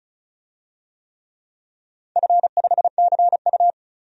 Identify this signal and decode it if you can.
Morse code, heard as a keyed tone, about 36 wpm, 700 Hz. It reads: F5CU
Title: DXCC-100 Call Signs - 4 Characters - 35wpm